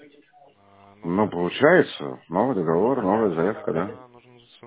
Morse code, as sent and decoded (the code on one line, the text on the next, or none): none